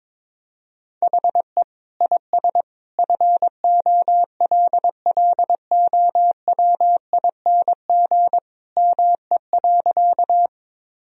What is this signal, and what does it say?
Morse code, heard as a keyed tone, about 21 wpm, 700 Hz. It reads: HE IS FOLLOWING ME.
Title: Sentences from Top 1000 Words - Review Part 4 - 22wpm